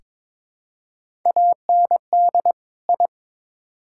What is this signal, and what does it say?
Morse code, heard as a keyed tone, about 22 wpm, 700 Hz. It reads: AND I
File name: Build A Phrase - 2-Word Phrases - 22wpm